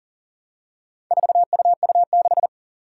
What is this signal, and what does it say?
Morse code, heard as a keyed tone, about 40 wpm, 700 Hz. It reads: 4UU6
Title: Four Challenging Letter-Number Combinations 40wpm